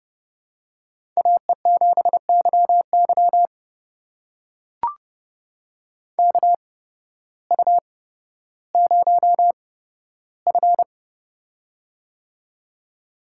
Morse code, heard as a keyed tone, about 28 wpm, spaced as for 14 wpm, 700 Hz. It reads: AE7YY KU0F
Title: USA Call Signs - 8x Character Spacing - 30wpm